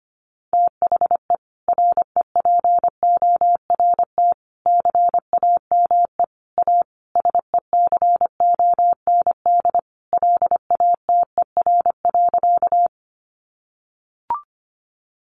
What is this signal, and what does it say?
Morse code, heard as a keyed tone, about 25 wpm, 700 Hz. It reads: THE REPORT CAME A SECOND LATER.
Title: Sentences from Top 900 Words 25wpm